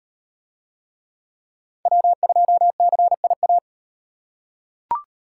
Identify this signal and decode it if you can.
Morse code, heard as a keyed tone, about 36 wpm, 700 Hz. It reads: W2CIA